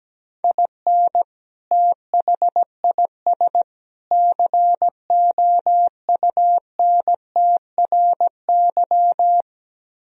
Morse code, heard as a keyed tone, about 17 wpm, 700 Hz. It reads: IN THIS COUNTRY
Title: Speed-Racing - Sets of 3 Words 17wpm